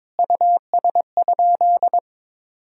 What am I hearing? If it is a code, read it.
Morse code, heard as a keyed tone, about 22 wpm, 700 Hz. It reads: US?
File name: practice / 22wpm / Sentences with 3-Word Phrases - 22wpm